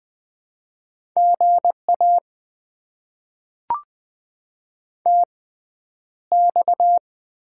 Morse code, heard as a keyed tone, about 20 wpm, spaced as for 10 wpm, 700 Hz. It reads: GA TX
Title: Top 90 QSO Elements - 6x Character Spacing - 20wpm